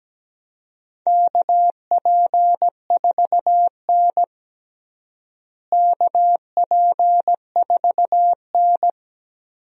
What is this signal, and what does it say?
Morse code, heard as a keyed tone, about 17 wpm, 700 Hz. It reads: KP4N KP4N
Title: DXCC-100 Call Signs - 4 Characters - Repeated 3x - 17wpm